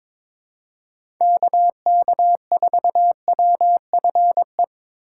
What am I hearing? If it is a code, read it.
Morse code, heard as a keyed tone, about 22 wpm, 700 Hz. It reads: KK4WFE
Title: USA Call Signs - 3x Character Spacing - 22wpm